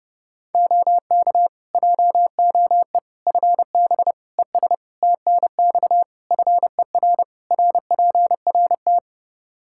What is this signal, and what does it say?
Morse code, heard as a keyed tone, about 30 wpm, 700 Hz. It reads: OK JOE FB ES TNX FER RPRT
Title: K1USN SST QSO Practice 30wpm